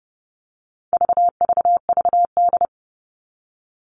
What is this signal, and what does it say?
Morse code, heard as a keyed tone, about 30 wpm, 700 Hz. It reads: VVVD
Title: All Cut Numbers - 4-Digit 30wpm